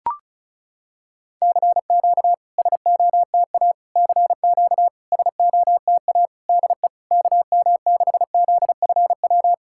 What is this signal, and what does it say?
Morse code, heard as a keyed tone, about 36 wpm, 700 Hz. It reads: CQ SOTA CQ SOTA DE KM6ZFW
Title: SOTA QSO Practice 35wpm